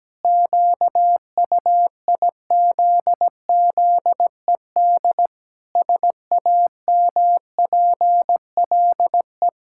Morse code, heard as a keyed tone, about 17 wpm, 700 Hz. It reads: QUIZZED SAMPLE
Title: Pangram Challenge v2 - 17wpm